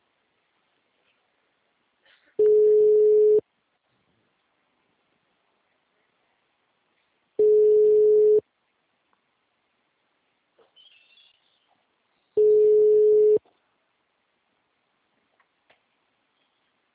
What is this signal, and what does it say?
Ringback tone (Europe, 425 Hz), with speech.